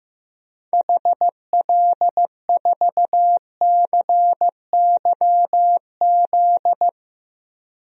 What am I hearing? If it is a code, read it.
Morse code, heard as a keyed tone, about 15 wpm, 700 Hz. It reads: HL4CYZ
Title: DXCC-100 Call Signs - 6 Characters - 15wpm